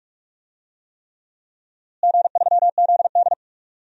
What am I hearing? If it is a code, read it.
Morse code, heard as a keyed tone, about 46 wpm, 700 Hz. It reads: G3ZD
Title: DXCC-100 Call Signs - 4 Characters - Repeated 3x - 45wpm